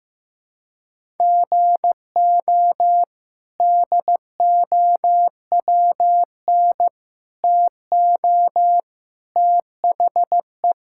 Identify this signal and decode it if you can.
Morse code, heard as a keyed tone, about 16 wpm, 700 Hz. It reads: GO DOWN TO THE
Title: Sentences from Top 1600 Words 15wpm